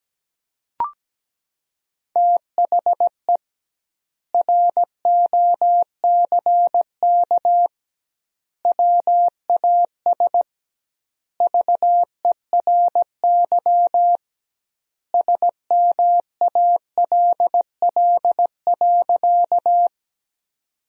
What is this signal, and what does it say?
Morse code, heard as a keyed tone, about 17 wpm, 700 Hz. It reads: THE ROCK WAS VERY SMALL.
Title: 2x Word-Spacing - Sentences from Top 400 Words - 17wpm